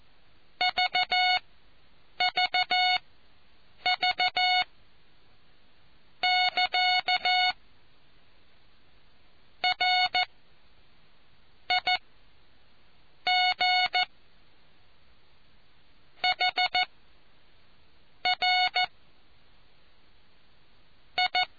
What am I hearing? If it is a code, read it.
Morse code, heard as a keyed tone, about 14 wpm, spaced as for 6 wpm, 730 Hz. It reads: VVV <KA> RIG HR I